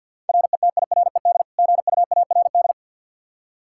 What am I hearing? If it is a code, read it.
Morse code, heard as a keyed tone, about 52 wpm, 700 Hz. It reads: RETIRED GUARD
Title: Practice QSO - Exchange 3 - More Info - 50wpm